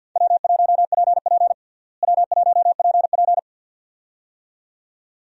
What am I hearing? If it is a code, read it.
Morse code, heard as a keyed tone, about 48 wpm, 700 Hz. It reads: W1PP W1PP
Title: USA Call Signs - 4 Characters - Repeated 2x - 50wpm